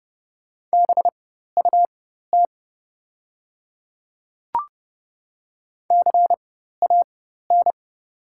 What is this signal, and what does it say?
Morse code, heard as a keyed tone, about 31 wpm, spaced as for 14 wpm, 690 Hz. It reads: BUT CAN
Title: 4x Character Spacing - Top 100 Words - 30wpm